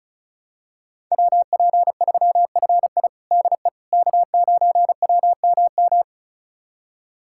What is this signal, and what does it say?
Morse code, heard as a keyed tone, about 34 wpm, 700 Hz. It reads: WP3FI DE K9WMM
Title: USA Callsign DE Callsign - 5 Characters - 35wpm